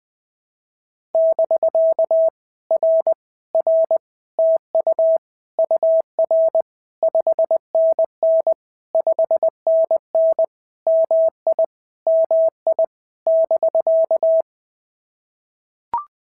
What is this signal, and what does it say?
Morse code, heard as a keyed tone, about 20 wpm, 650 Hz. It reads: <BK> R R TU UR 5NN 5NN MI MI <BK>
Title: POTA QSO Practice - 20wpm